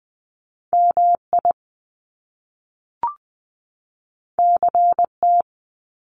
Morse code, heard as a keyed tone, about 20 wpm, 700 Hz. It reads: MI CT
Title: US States 20wpm